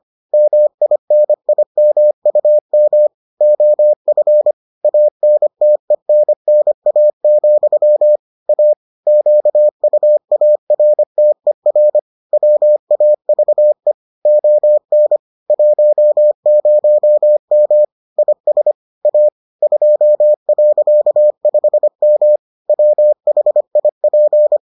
Morse code, heard as a keyed tone, about 25 wpm, 600 Hz. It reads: MINIMUM OF ANTENNA, A QUARTER WAVE ON 10M IS A 2.5M WHIP